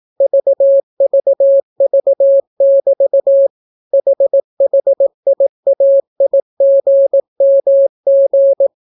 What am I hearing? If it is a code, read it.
Morse code, heard as a keyed tone, about 18 wpm, 550 Hz. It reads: VVV<BT> HHIAIGMG